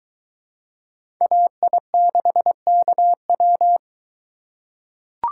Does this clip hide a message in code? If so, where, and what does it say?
Yes: Morse code from 1.2 to 3.8 s, heard as a keyed tone, about 22 wpm, 700 Hz. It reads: AI6KW